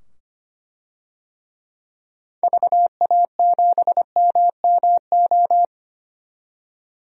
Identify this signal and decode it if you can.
Morse code, heard as a keyed tone, about 25 wpm, 700 Hz. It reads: VA7MMO
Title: DXCC-100 Call Signs - 6 Characters - 25wpm